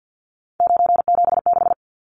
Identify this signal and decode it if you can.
Morse code, heard as a keyed tone, about 50 wpm, 700 Hz. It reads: 976
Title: Numbers - 3-Digit 50wpm